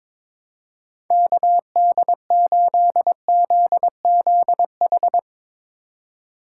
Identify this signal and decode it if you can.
Morse code, heard as a keyed tone, about 21 wpm, 700 Hz. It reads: KD8ZZH
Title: USA Call Signs - 6 Characters - 4x Character Spacing - 22wpm